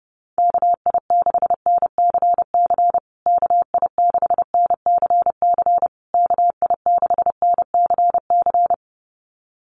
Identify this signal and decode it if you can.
Morse code, heard as a keyed tone, about 30 wpm, 700 Hz. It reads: KI6NCC KI6NCC KI6NCC
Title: USA Call Signs Repeated 3x 30wpm